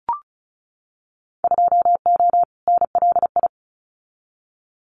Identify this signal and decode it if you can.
Morse code, heard as a keyed tone, about 34 wpm, 700 Hz. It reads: 2O NLI
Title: Winter Field Day Exchanges 35wpm